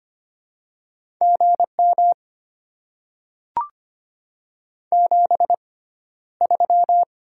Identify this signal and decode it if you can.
Morse code, heard as a keyed tone, about 25 wpm, spaced as for 13 wpm, 690 Hz. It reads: GM 73
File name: Top 20 QSO Elements - 6x Character Spacing - 25wpm